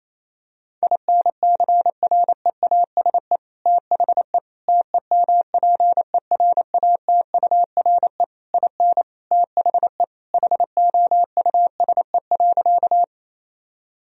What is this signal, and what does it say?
Morse code, heard as a keyed tone, about 28 wpm, 700 Hz. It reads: INCREASE THE TEMPERATURE IN THE HOUSE.
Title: Sentences from Top 1000 Words - Review Part 4 - 28wpm